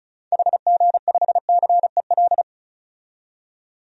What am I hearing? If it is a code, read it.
Morse code, heard as a keyed tone, about 36 wpm, 700 Hz. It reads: HG5CEL
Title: DXCC-100 Call Signs - 6 Characters - 35wpm